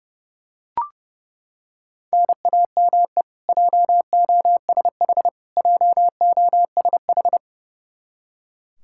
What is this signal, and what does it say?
Morse code, heard as a keyed tone, about 30 wpm, 700 Hz. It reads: NAME JOSH JOSH
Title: K1USN SST QSO Practice 30wpm